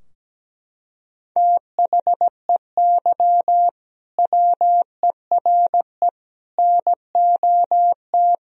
Morse code, heard as a keyed tone, about 17 wpm, 700 Hz. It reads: THEY WERE NOT
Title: Sentences from Top 1000 Words - Review Part 3 - 17wpm